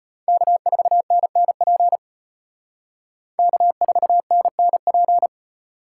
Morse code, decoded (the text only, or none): K4NNP K4NNP